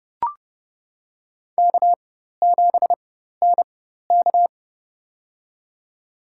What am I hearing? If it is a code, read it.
Morse code, heard as a keyed tone, about 30 wpm, spaced as for 14 wpm, 700 Hz. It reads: K7NK
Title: USA Call Signs - 4 Characters - 4x Character Spacing - 30wpm